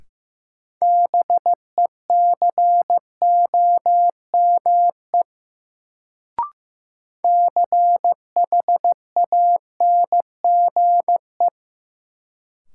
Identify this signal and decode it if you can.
Morse code, heard as a keyed tone, about 15 wpm, 700 Hz. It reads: BECOME CHANGE
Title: Sentences from Top 600 Words - New Word Drill - 15wpm